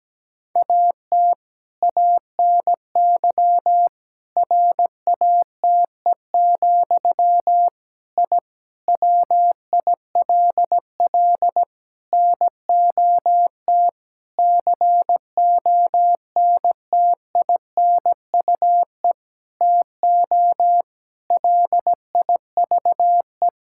Morse code, heard as a keyed tone, about 17 wpm, 700 Hz. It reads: AT ANY RATE, I WILL NOT CONTINUE TO LIVE
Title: Sentences from Top 1000 Words - Review Part 6 - 17wpm